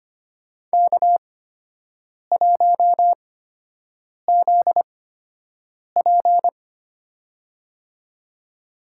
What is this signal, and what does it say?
Morse code, heard as a keyed tone, about 25 wpm, spaced as for 7 wpm, 700 Hz. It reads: K1ZP
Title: USA Call Signs - 4 Characters - 8x Character Spacing - 25wpm